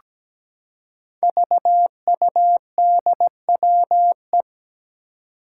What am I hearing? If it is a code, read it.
Morse code, heard as a keyed tone, about 17 wpm, 700 Hz. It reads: VUDWE
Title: Five Challenging Letter-Number Combinations 17wpm